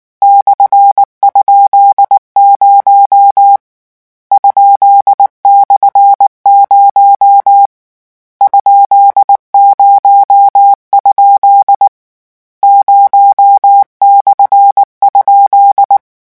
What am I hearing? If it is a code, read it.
Morse code, heard as a keyed tone, about 19 wpm, 800 Hz. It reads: /?0 ?/0 ?0? 0/?
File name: morse mp3 / lecon13-vitesse 18